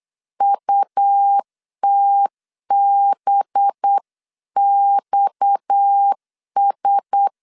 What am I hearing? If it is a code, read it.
Morse code, heard as a keyed tone, about 8 wpm, 800 Hz. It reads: UTBXS
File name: 8wpm